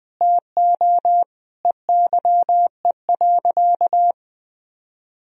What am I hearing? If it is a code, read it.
Morse code, heard as a keyed tone, about 21 wpm, 700 Hz. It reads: TO EYE.
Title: Sentences from Top 1000 Words - Review Part 2 - 20wpm